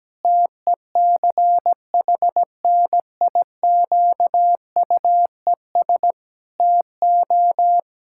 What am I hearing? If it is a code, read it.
Morse code, heard as a keyed tone, about 17 wpm, 700 Hz. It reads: TECHNIQUES TO
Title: Pangram Challenge v2 - 17wpm